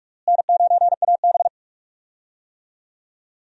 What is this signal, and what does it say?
Morse code, heard as a keyed tone, about 42 wpm, 700 Hz. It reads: N9AB
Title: USA Call Signs - 4 Characters - Repeated 2x - 45wpm